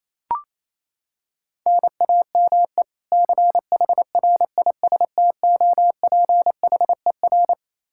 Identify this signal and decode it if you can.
Morse code, heard as a keyed tone, about 28 wpm, 700 Hz. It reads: NAME CHRISTOPHER
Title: Practice QSO - Exchange 1 - Beginning - 28wpm